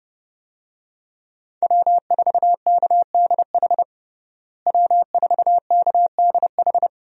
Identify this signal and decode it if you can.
Morse code, heard as a keyed tone, about 30 wpm, 700 Hz. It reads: W4KDH W4KDH
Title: USA Call Signs - 5 Characters - Repeated 2x - 30wpm